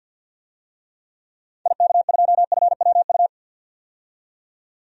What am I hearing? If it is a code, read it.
Morse code, heard as a keyed tone, about 47 wpm, 700 Hz. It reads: IK2FWU
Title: DXCC-100 Call Signs - 6 Characters - 50wpm